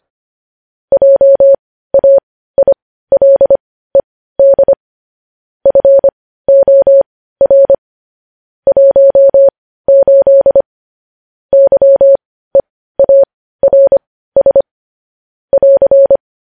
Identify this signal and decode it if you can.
Morse code, heard as a keyed tone, about 25 wpm, spaced as for 15 wpm, 550 Hz. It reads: JAILED FOR 18 YEARS <AR>
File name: podcast / News headlines-1x-25x15